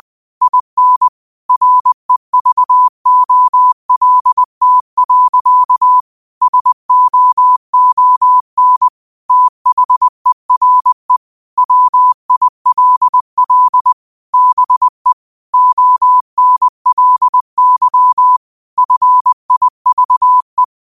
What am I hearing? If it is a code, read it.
Morse code, heard as a keyed tone, about 20 wpm, 1000 Hz. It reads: IN REVOLT. SOON THERE WILL BE ONLY FIVE
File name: qotdfiles / 20wpm